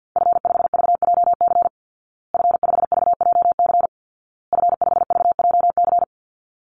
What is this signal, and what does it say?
Morse code, heard as a keyed tone, about 50 wpm, 700 Hz. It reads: F5VJC F5VJC F5VJC
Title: Call Signs Repeated 2x - 50wpm